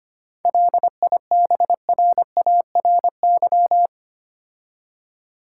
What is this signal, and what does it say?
Morse code, heard as a keyed tone, about 26 wpm, 700 Hz. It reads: LIBRARY
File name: Easy Words - Speed-Racing with Hints - 25wpm